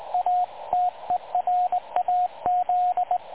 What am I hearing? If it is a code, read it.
Morse code, heard as a keyed tone, about 20 wpm, 720 Hz, with radio noise. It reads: A TERAZ
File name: dlazaawansowanych / nagraniazpasma / websdr recording 3565.0kHz